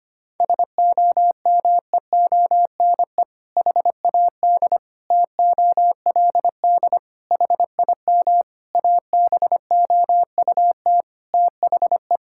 Morse code, heard as a keyed tone, about 25 wpm, 700 Hz. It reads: SOMEONE HAD TOLD HIM ABOUT THE
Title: Sentences from Top 1000 Words - Review Part 6 - 25wpm